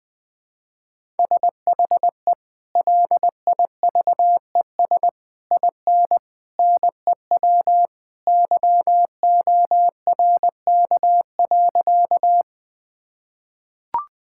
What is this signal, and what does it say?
Morse code, heard as a keyed tone, about 20 wpm, 700 Hz. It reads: SHE LIVES IN NEW YORK.
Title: Sentences with 3-Word Phrases - 20wpm